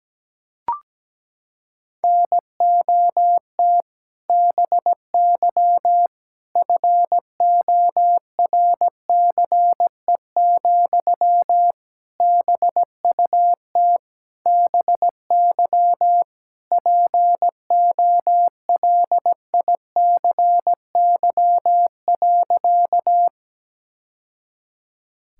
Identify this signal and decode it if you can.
Morse code, heard as a keyed tone, about 17 wpm, 700 Hz. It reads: NOT BY FORCE, BUT BY POLICY.
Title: Sentences from Top 1000 Words - Review Part 5 - 17wpm